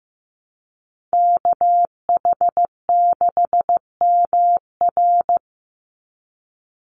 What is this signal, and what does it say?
Morse code, heard as a keyed tone, about 15 wpm, 700 Hz. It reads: KH6MR
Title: USA Call Signs 15wpm